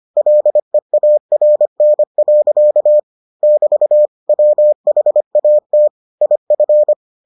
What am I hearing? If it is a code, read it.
Morse code, heard as a keyed tone, about 25 wpm, 600 Hz. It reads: LEARN. <BT> WHAT IF